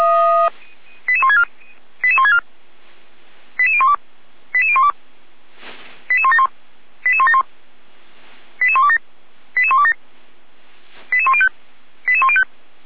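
Five-tone selective call (ZVEI), heard as quick runs of five tones, tones of about 70 ms. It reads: 88164 88164 88012 88012 88171 88171 88127 88127 88105 88105
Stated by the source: Five-tone ZVEI selective call with code 88314, received by SDR.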